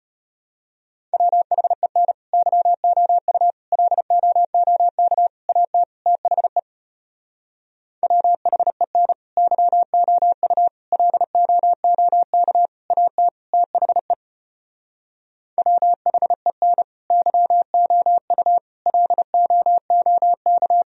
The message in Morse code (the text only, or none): WHEN YOU LOOK AT THE WHEN YOU LOOK AT THE WHEN YOU LOOK